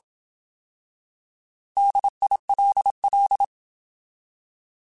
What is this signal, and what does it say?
Morse code, heard as a keyed tone, about 27 wpm, 800 Hz. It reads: DILL
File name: Alphabetical / Alphabetical mp3s / Ds @25